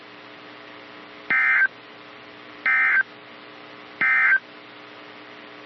Emergency Alert System (SAME) tones, heard as harsh buzzing bursts. It is heard from 1.2 to 4.5 s.
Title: NOAA Weather Radio - KEC55